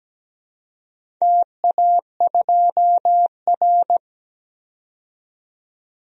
Morse code, heard as a keyed tone, about 17 wpm, 700 Hz. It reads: TA2R
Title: DXCC-100 Call Signs - 4 Characters - 17wpm